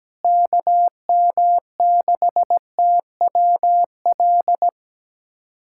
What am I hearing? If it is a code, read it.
Morse code, heard as a keyed tone, about 17 wpm, 700 Hz. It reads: KM6TWL
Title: USA Callsign DE Callsign 17wpm